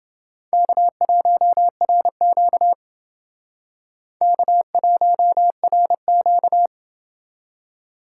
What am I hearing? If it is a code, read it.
Morse code, heard as a keyed tone, about 31 wpm, 700 Hz. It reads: K1RQ K1RQ